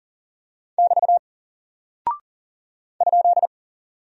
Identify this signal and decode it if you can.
Morse code, heard as a keyed tone, about 41 wpm, 700 Hz. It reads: <BT> ?
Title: Single Character (Letter:Number:Punctuation:Prosign) Alt 40wpm